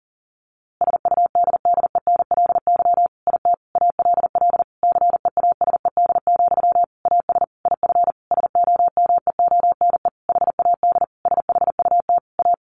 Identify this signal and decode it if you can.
Morse code, heard as a keyed tone, about 40 wpm, 700 Hz. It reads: SUDDENLY IT ALL CEASED, AS IF SOMEONE HAD SHUT A
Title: A Sound Of Thunder 40wpm